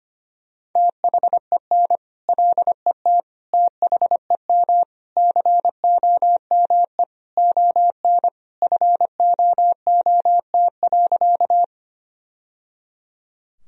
Morse code, heard as a keyed tone, about 25 wpm, 700 Hz. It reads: THEN LET THEM COME ON FOOT.
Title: Sentences from Top 1000 Words - Review Part 3 - 25wpm